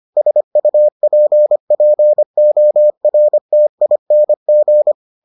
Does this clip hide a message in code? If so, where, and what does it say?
Yes: Morse code from 0.2 to 4.9 s, heard as a keyed tone, about 25 wpm, 600 Hz. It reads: SUPPORTING